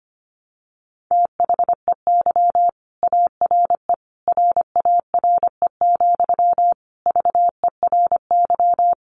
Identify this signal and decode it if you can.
Morse code, heard as a keyed tone, about 25 wpm, 700 Hz. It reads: THEY ARE RARE, VERY
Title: Sentences from Top 1100 Words 25wpm